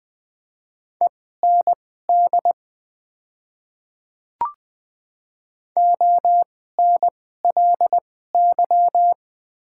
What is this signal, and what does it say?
Morse code, heard as a keyed tone, about 20 wpm, spaced as for 14 wpm, 700 Hz. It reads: END ONLY
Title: 2x Character Spacing - Top 300 Words - 20wpm